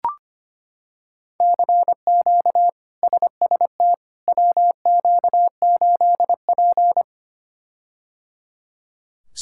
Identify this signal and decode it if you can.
Morse code, heard as a keyed tone, about 25 wpm, 700 Hz. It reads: CQ SST WQ8P
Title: K1USN SST QSO Practice - 25wpm